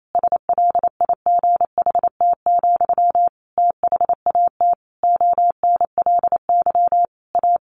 Morse code, heard as a keyed tone, about 28 wpm, 700 Hz. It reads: SLIGHT, THAT ONLY A
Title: A Sound Of Thunder 28wpm